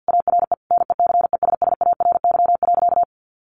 Morse code, heard as a keyed tone, about 50 wpm, 700 Hz. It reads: ARE NECESSARY.
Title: Sentences from Top 1400 Words 50wpm